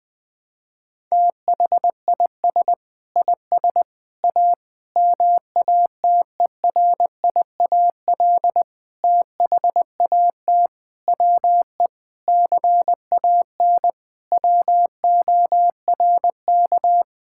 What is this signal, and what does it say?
Morse code, heard as a keyed tone, about 20 wpm, 700 Hz. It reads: THIS IS A MATERIAL THAT WE CAN WORK
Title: Sentences from Top 1000 Words - Review Part 5 - 20wpm